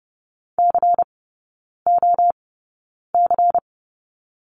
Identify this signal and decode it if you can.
Morse code, heard as a keyed tone, about 30 wpm, spaced as for 9 wpm, 700 Hz. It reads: COC
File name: Lesson 16e - Letter c - Three Characters 30wpm